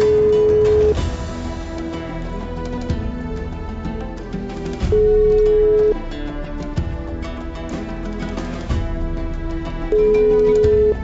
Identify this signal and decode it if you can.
Ringback tone (Europe, 425 Hz), with noise.